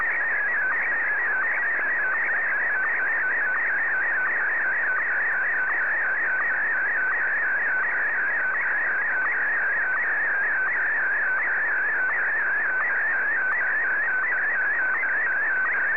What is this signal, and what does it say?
SSTV (slow-scan television), PD 90.